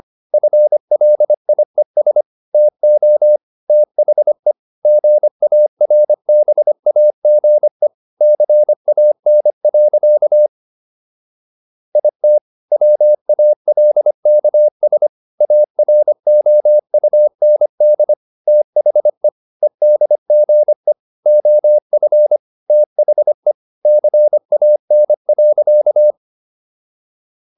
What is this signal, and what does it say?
Morse code, heard as a keyed tone, about 25 wpm, 600 Hz. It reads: FLIES TO THE GARBAGE CAN. IT WALKS AROUND THE EDGE OF THE CAN.